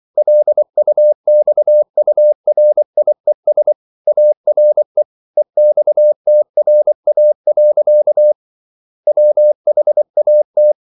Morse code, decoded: LUXURIES ARE EXTRA. WHAT

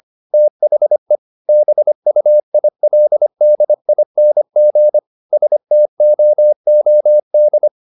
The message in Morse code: THE BUILDING STOOD

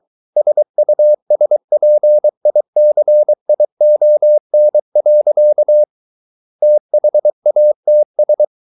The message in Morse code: SUSPICION. THATS

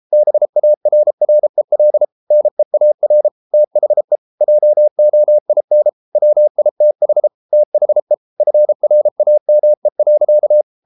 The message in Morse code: BARREL NEAR THE JOIN WITH THE FRAME.

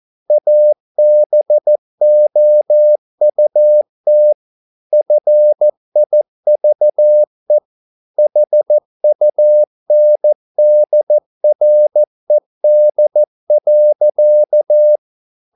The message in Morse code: ABOUT FIVE HUNDRED.

